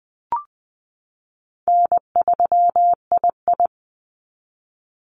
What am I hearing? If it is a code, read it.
Morse code, heard as a keyed tone, about 20 wpm, 700 Hz. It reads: N3II
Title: NA SOTA Activator Callsigns 20wpm